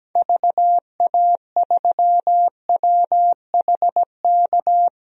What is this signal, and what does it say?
Morse code, heard as a keyed tone, about 17 wpm, 700 Hz. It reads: VA3WHK
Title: DXCC-100 Call Signs - 6 Characters - Repeated 2x - 17wpm